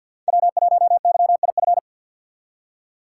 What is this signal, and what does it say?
Morse code, heard as a keyed tone, about 47 wpm, 700 Hz. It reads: W1YIF
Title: USA Call Signs - 4x Character Spacing - 50wpm